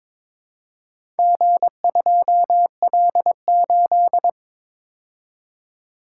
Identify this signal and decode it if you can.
Morse code, heard as a keyed tone, about 22 wpm, 700 Hz. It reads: G2L8
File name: practice / 22wpm / Four Letter-Number Combinations - 22wpm